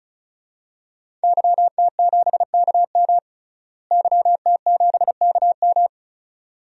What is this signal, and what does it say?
Morse code, heard as a keyed tone, about 34 wpm, 700 Hz. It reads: YT7KM YT7KM
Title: DXCC-100 Call Signs - 5 Characters - Repeated 2x - 35wpm